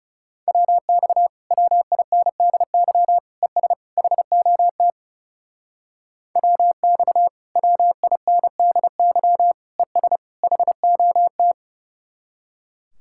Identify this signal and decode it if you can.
Morse code, heard as a keyed tone, about 38 wpm, 700 Hz. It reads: WX WINDY ES HOT WX WINDY ES HOT